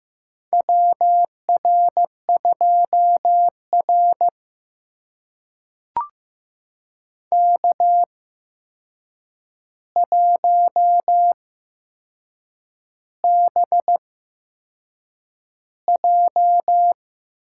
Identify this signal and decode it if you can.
Morse code, heard as a keyed tone, about 15 wpm, spaced as for 6 wpm, 700 Hz. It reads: WR2R K1BJ
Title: USA Call Signs - 4 Characters - 8x Character Spacing - 15wpm